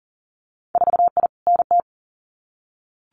Morse code, heard as a keyed tone, about 38 wpm, 700 Hz. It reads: 4I NT
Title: Winter Field Day Exchanges Encore 40wpm